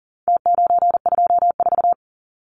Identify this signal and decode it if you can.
Morse code, heard as a keyed tone, about 40 wpm, 700 Hz. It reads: T924
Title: Numbers - Contest Style - 4-Digit 40wpm